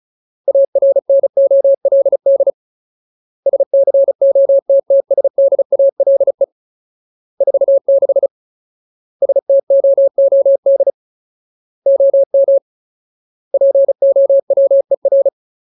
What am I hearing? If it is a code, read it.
Morse code, heard as a keyed tone, about 34 wpm, 550 Hz. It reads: ARNOLD SCOTTSDALE 46 STOOD OM POWER
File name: Ultimate Word List / 35WPM Ultimate